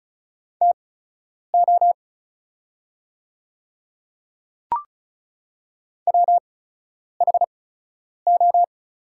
Morse code, heard as a keyed tone, about 36 wpm, spaced as for 10 wpm, 690 Hz. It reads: TO WHO
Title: 8x Character Spacing - Top 200 Words - 35wpm